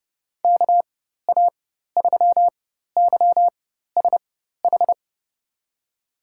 Morse code, heard as a keyed tone, about 30 wpm, spaced as for 14 wpm, 700 Hz. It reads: KA3YSH
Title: USA Call Signs - 6 Characters - 4x Character Spacing - 30wpm